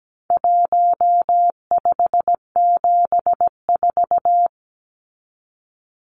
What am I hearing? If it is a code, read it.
Morse code, heard as a keyed tone, about 17 wpm, 700 Hz. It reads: 1574
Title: Numbers - 4-Digit 17wpm